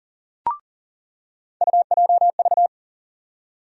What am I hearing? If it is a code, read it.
Morse code, heard as a keyed tone, about 40 wpm, 700 Hz. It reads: UJV
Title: USA Call Sign Postfixes - 3 Letters 40wpm